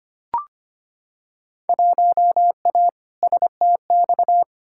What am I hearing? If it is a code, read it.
Morse code, heard as a keyed tone, about 25 wpm, 700 Hz. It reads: 1A STX
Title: Field Day Exchanges Encore - 25wpm